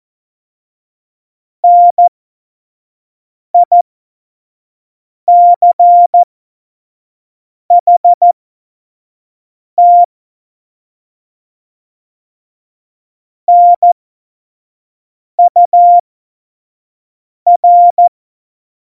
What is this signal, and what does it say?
Morse code, heard as a keyed tone, about 14 wpm, spaced as for 5 wpm, 700 Hz. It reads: NICHT NUR
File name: Morsekurs / Witze / Witz 27. Oktober 2020, 25 F70